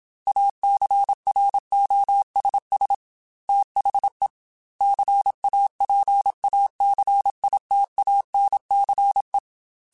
Morse code, heard as a keyed tone, about 27 wpm, 800 Hz. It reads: ACROSS THE CAPACITANCE
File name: Technical Subjects / Technical Subjects-mp3s / XC @25